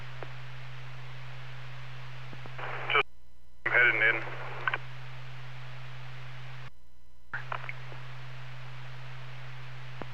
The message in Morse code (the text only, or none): none